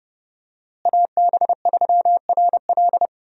Morse code, heard as a keyed tone, about 32 wpm, 700 Hz. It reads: AB3RL
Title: USA Call Signs - 5 Characters - Repeated 3x - 30wpm